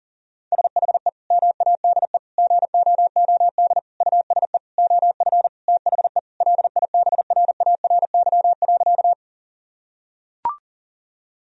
Morse code, heard as a keyed tone, about 40 wpm, 700 Hz. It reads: SHE MADE GOOD USE OF THE LIBRARY.
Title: Sentences from Top 1000 Words - Review Part 4 - 40wpm